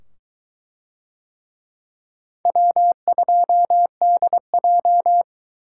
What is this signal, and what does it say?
Morse code, heard as a keyed tone, about 22 wpm, 700 Hz. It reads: W2DJ